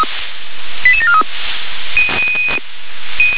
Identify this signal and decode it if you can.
Five-tone selective call (ZVEI), heard as quick runs of five tones, tones of about 70 ms. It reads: 88743
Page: ZVEI